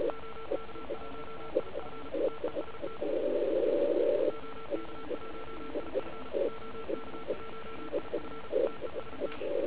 Morse code, heard as a keyed tone, about 34 wpm, 1340 Hz, with noise. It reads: OK1IF TEST OK1IF OK1IF TEST